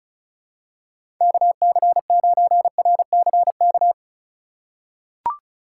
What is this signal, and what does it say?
Morse code, heard as a keyed tone, about 34 wpm, 700 Hz. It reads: KC9RCK
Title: USA Call Signs - 6 Characters - 2x Character Spacing - 35wpm